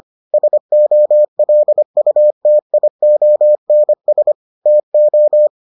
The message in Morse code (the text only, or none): SOLUTIONS TO